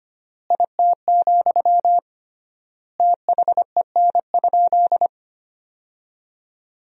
Morse code, heard as a keyed tone, about 25 wpm, 700 Hz. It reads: IT, THEN?
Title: 3x Word-Spacing - Sentences from Top 200 Words - 25wpm